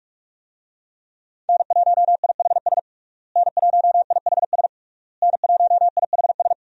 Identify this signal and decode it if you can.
Morse code, heard as a keyed tone, about 46 wpm, 700 Hz. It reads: N1IHS N1IHS N1IHS
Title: USA Call Signs - 5 Characters - Repeated 3x - 45wpm